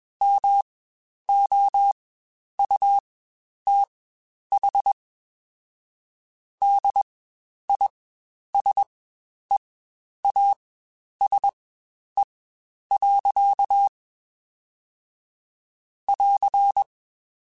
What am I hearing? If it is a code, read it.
Morse code, heard as a keyed tone, about 21 wpm, spaced as for 10 wpm, 800 Hz. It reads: MOUTH DISEASE. <AR>
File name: Quotes Spaced / Quotes Spaced-mp3s / HoofInMouth sp @20